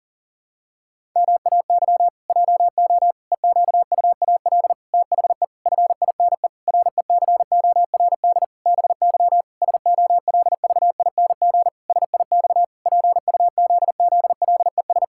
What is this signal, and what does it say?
Morse code, heard as a keyed tone, about 40 wpm, 700 Hz. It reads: MAY JO EQUAL THE FINE RECORD BY SOLVING SIX PUZZLES
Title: Pangram Challenge v2 - 40wpm